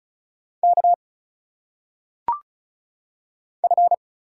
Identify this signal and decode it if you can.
Morse code, heard as a keyed tone, about 34 wpm, 700 Hz. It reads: K F
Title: Single Letters - Phonetic Pronunciation 35wpm